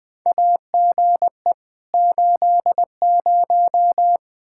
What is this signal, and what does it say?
Morse code, heard as a keyed tone, about 20 wpm, 700 Hz. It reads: AGE 80
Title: K1USN SST QSO Practice 20wpm